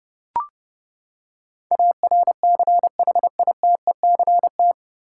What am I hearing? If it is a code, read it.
Morse code, heard as a keyed tone, about 30 wpm, 700 Hz. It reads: ARCHITECT
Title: K1USN SST QSO Practice 30wpm